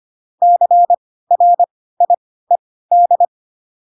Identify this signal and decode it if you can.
Morse code, heard as a keyed tone, about 25 wpm, spaced as for 16 wpm, 700 Hz. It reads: CRIED
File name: Pinocchio / Pinocchio ch4 10 25 ik1hge